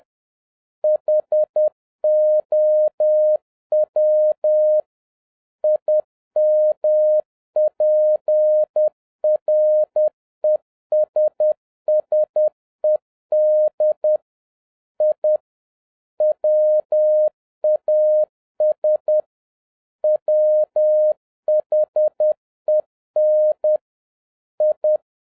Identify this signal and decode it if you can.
Morse code, heard as a keyed tone, about 10 wpm, 610 Hz. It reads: HOW IMPRESSED I WAS WHEN I